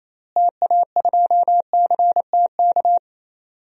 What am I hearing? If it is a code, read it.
Morse code, heard as a keyed tone, about 28 wpm, 700 Hz. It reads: TA2CTK
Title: DXCC-100 Call Signs - 6 Characters - Repeated 3x - 28wpm